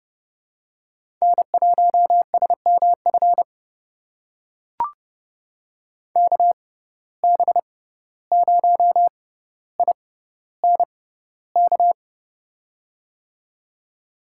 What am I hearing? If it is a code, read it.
Morse code, heard as a keyed tone, about 30 wpm, spaced as for 15 wpm, 700 Hz. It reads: N1SMF KB0INK
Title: USA Call Signs - 6x Character Spacing - 30wpm